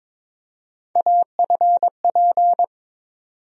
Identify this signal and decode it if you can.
Morse code, heard as a keyed tone, about 22 wpm, 700 Hz. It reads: AFP
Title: USA Call Sign Postfixes - 3 Letters 22wpm